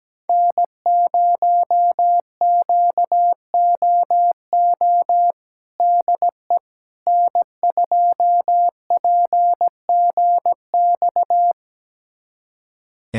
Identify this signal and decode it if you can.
Morse code, heard as a keyed tone, about 17 wpm, 700 Hz. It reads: N0QOO DE N2PGX
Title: USA Callsign DE Callsign - 5 Characters - 17wpm